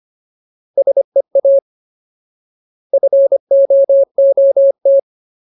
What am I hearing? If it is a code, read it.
Morse code, heard as a keyed tone, about 25 wpm, 550 Hz. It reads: SEA FOOT
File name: Ultimate Word List / 25WPM Ultimate